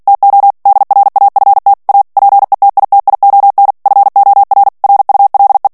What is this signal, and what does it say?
Morse code, heard as a keyed tone, about 48 wpm, 800 Hz. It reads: TO DRAFT A PETITION FOR RULE
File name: ANGLAIS texte clair 3 vitesse 45 mots minute